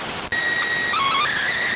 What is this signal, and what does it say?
SSTV (slow-scan television), PD 290.